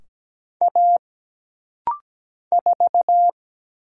Morse code, heard as a keyed tone, about 17 wpm, 700 Hz. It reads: A 4
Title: Single Letter-Number Alt - Rapid-Fire 17wpm